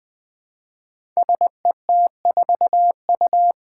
Morse code, heard as a keyed tone, about 20 wpm, 700 Hz. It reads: SET4U
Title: Five Challenging Letter-Number Combinations 20wpm